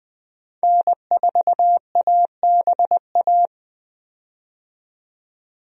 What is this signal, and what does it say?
Morse code, heard as a keyed tone, about 20 wpm, 700 Hz. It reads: N4ABA
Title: USA Call Signs - 5 Characters - 20wpm